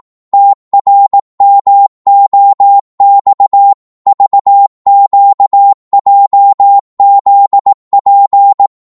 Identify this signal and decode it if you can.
Morse code, heard as a keyed tone, about 18 wpm, 800 Hz. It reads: TRMOX VQJZP
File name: Morse / ltr-18wpm